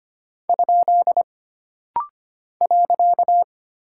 Morse code, heard as a keyed tone, about 25 wpm, 700 Hz. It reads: ? .
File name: Single Punctuation-Prosign - Rapid-Fire - 25wpm